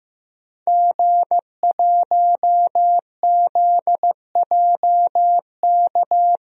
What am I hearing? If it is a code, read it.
Morse code, heard as a keyed tone, about 15 wpm, 700 Hz. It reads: G1ZJK
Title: DXCC-100 Call Signs - 5 Characters - Repeated 3x - 15wpm